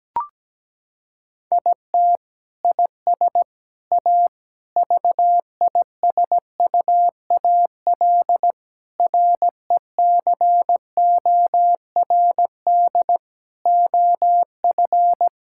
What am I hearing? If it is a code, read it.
Morse code, heard as a keyed tone, about 17 wpm, 700 Hz. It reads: IT IS A VISUAL RECORD OF
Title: Sentences from Top 1000 Words - Review Part 6 - 17wpm